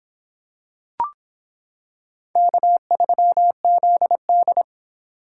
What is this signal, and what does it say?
Morse code, heard as a keyed tone, about 26 wpm, 700 Hz. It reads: K3ZD